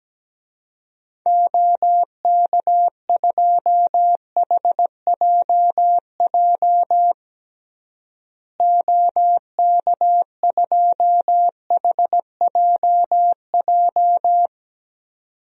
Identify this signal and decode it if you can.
Morse code, heard as a keyed tone, about 17 wpm, 700 Hz. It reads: OK2HJJ OK2HJJ
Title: DXCC-100 Call Signs - 6 Characters - Repeated 2x - 17wpm